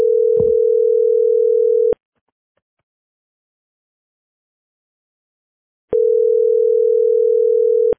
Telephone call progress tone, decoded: ringback tone